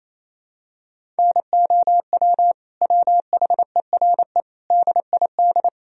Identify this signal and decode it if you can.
Morse code, heard as a keyed tone, about 28 wpm, 700 Hz. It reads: NOW WHERE DID
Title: Sentences from Top 200 Words and Abbreviations 28wpm